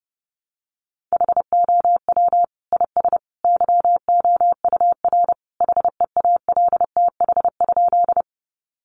Morse code, heard as a keyed tone, about 30 wpm, 700 Hz. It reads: HOW IS YOUR HEALTH?
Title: Sentences from Top 600 Words 30wpm